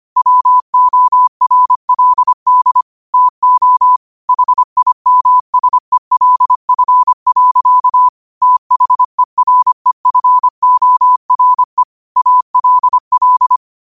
Morse code, heard as a keyed tone, about 25 wpm, 1000 Hz. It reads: WORLD TO HIMSELF. THEREFORE ALL
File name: qotdfiles / 25wpm